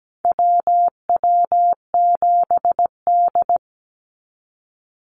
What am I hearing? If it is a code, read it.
Morse code, heard as a keyed tone, about 17 wpm, 700 Hz. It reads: WW7D
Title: NA SOTA Activator Callsigns 17wpm